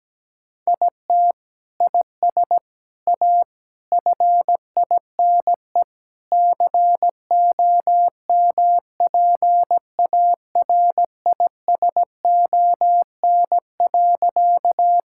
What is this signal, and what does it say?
Morse code, heard as a keyed tone, about 17 wpm, 700 Hz. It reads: IT IS A FINE COMPARISON.
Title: Sentences from Top 1600 Words 17wpm